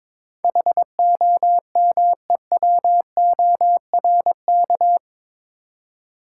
Morse code, heard as a keyed tone, about 22 wpm, 700 Hz. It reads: HOMEWORK
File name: practice / 22wpm / Eight Letter Words - 22wpm